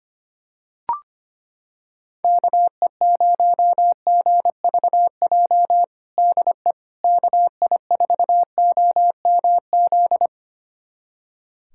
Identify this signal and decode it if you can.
Morse code, heard as a keyed tone, about 25 wpm, 700 Hz. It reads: KE0GVJ DE KI4OMZ
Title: USA Callsign DE Callsign 25wpm